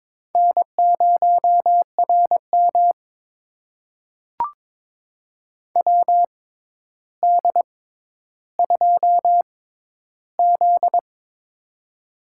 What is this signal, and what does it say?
Morse code, heard as a keyed tone, about 22 wpm, spaced as for 11 wpm, 700 Hz. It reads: N0RM WD2Z
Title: USA Call Signs - 4 Characters - 6x Character Spacing - 22wpm